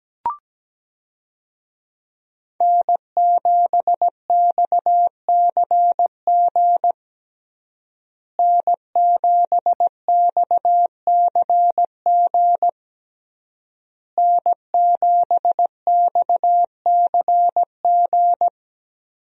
Morse code, heard as a keyed tone, about 17 wpm, 700 Hz. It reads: N7XCG N7XCG N7XCG
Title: USA Call Signs - 5 Characters - Repeated 3x - 17wpm